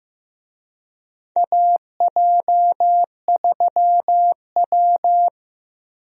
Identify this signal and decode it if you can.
Morse code, heard as a keyed tone, about 15 wpm, 700 Hz. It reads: AJ3W
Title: USA Call Signs - 4 Characters - Repeated 2x - 15wpm